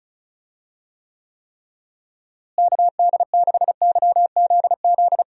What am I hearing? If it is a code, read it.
Morse code, heard as a keyed tone, about 36 wpm, 700 Hz. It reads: KD6YZZ
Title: USA Call Signs - 6 Characters - Repeated 3x - 35wpm